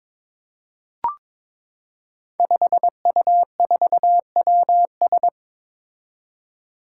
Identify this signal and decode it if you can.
Morse code, heard as a keyed tone, about 22 wpm, 700 Hz. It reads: 5U4WS
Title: Five Challenging Letter-Number Combinations 22wpm